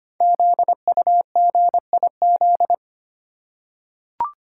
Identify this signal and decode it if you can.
Morse code, heard as a keyed tone, about 25 wpm, 700 Hz. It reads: ZUGIZ
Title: Code Groups - 5 Letter:Numbers - 25wpm